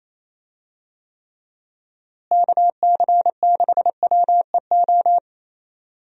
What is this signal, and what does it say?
Morse code, heard as a keyed tone, about 28 wpm, 700 Hz. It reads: KC6WEO
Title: USA Call Signs - 6 Characters - Repeated 2x - 28wpm